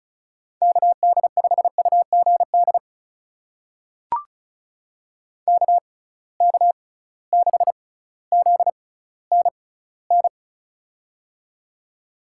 Morse code, heard as a keyed tone, about 34 wpm, spaced as for 18 wpm, 700 Hz. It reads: KD5UGD KK6ZNN
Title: USA Call Signs - 6 Characters - 6x Character Spacing - 35wpm